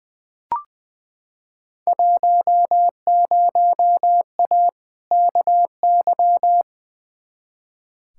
Morse code, heard as a keyed tone, about 20 wpm, 700 Hz. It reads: 10A KY
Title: Field Day Exchanges Encore - 20wpm